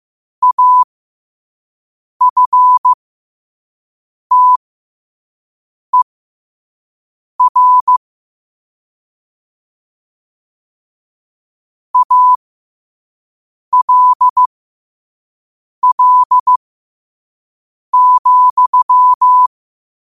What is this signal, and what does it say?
Morse code, heard as a keyed tone, about 15 wpm, spaced as for 5 wpm, 1000 Hz. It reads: AFTER ALL,
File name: qotdfiles / 5wpm